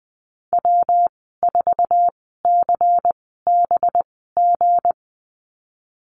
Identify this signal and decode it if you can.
Morse code, heard as a keyed tone, about 20 wpm, spaced as for 14 wpm, 700 Hz. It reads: W4CBG
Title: 219 Lesson 35g - Number 0 - Call Signs 20wpm